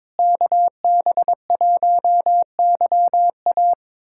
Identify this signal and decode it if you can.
Morse code, heard as a keyed tone, about 22 wpm, 700 Hz. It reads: KB1YA
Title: USA Callsign DE Callsign 22wpm